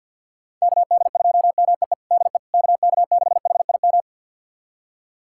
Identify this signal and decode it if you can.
Morse code, heard as a keyed tone, about 47 wpm, 700 Hz. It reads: KD2MEE DE KK6HIM
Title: USA Callsign DE Callsign - 6 Characters - 50wpm